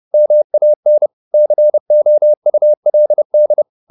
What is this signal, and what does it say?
Morse code, heard as a keyed tone, about 30 wpm, 600 Hz. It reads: MAN COULD